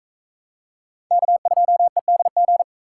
Morse code, heard as a keyed tone, about 41 wpm, 700 Hz. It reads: K2EDG